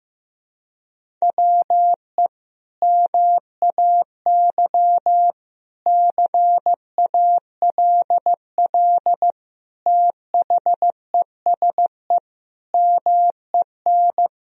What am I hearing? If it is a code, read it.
Morse code, heard as a keyed tone, about 15 wpm, 700 Hz. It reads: WE MAY CALL THESE MEN
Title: Sentences from Top 1000 Words - Review Part 1 - 15wpm